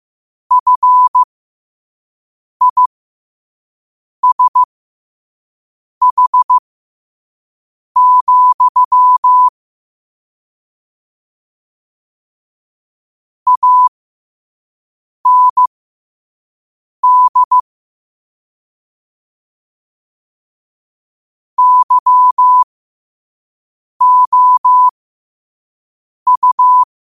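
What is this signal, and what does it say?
Morse code, heard as a keyed tone, about 15 wpm, spaced as for 5 wpm, 1000 Hz. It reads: FISH, AND YOU
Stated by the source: Morse code quote of the day sent at 5 wpm.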